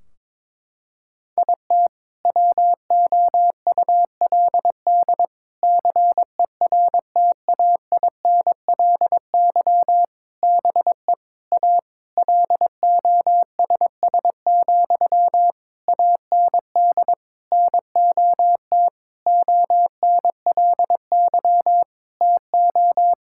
Morse code, heard as a keyed tone, about 21 wpm, 700 Hz. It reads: IT WOULD CERTAINLY BE A LOSS, AND NOT ONLY TO
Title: Sentences from Top 1600 Words 22wpm